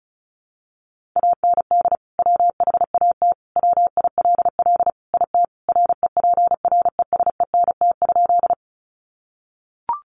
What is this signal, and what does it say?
Morse code, heard as a keyed tone, about 34 wpm, 700 Hz. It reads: AND WHAT WILL IT REPRESENT?
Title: Sentences from Top 1200 Words 35wpm